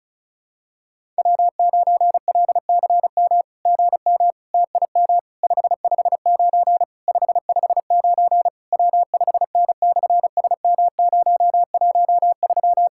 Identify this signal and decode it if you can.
Morse code, heard as a keyed tone, about 33 wpm, 700 Hz. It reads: W9LCM GM TIM 559 559 W5N/SM013
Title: SOTA QSO Practice 35wpm